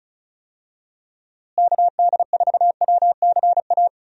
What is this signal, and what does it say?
Morse code, heard as a keyed tone, about 34 wpm, 700 Hz. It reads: KD4WCA
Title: USA Call Signs - 6 Characters - Repeated 2x - 35wpm